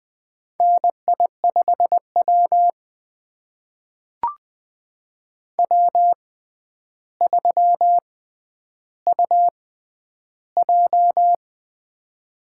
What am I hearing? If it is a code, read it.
Morse code, heard as a keyed tone, about 20 wpm, spaced as for 10 wpm, 700 Hz. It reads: NI5W W3UJ
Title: USA Call Signs - 4 Characters - 6x Character Spacing - 20wpm